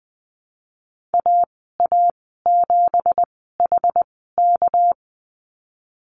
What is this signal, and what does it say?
Morse code, heard as a keyed tone, about 19 wpm, spaced as for 14 wpm, 700 Hz. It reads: AA7HK
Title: 219 Lesson 35g - Number 0 - Call Signs 20wpm